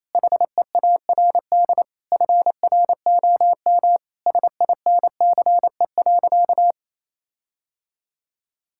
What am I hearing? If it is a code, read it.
Morse code, heard as a keyed tone, about 28 wpm, 700 Hz. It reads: HEARD FROM SINCE.
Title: Sentences from Top 1600 Words 28wpm